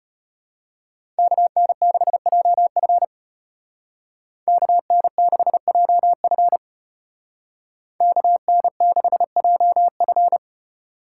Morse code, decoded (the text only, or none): KN6JF KN6JF KN6JF